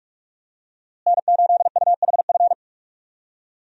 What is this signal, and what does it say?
Morse code, heard as a keyed tone, about 41 wpm, 700 Hz. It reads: N8UHF
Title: USA Call Signs - 5 Characters - Repeated 2x - 45wpm